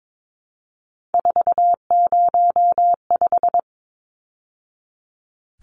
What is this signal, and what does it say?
Morse code, heard as a keyed tone, about 22 wpm, 700 Hz. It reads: 405
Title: Numbers - 3-Digit 22wpm